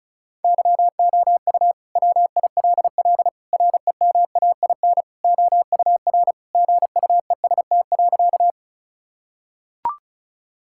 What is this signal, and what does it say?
Morse code, heard as a keyed tone, about 34 wpm, 700 Hz. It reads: YOU WILL REMAIN OUR GUEST.
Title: Sentences from Top 1600 Words 35wpm